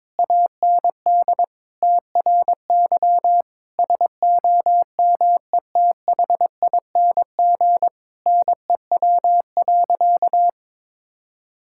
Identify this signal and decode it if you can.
Morse code, heard as a keyed tone, about 22 wpm, 700 Hz. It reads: AND TRY SOMETHING NEW.
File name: practice / 22wpm / Sentences with 8-Word Phrases - 22wpm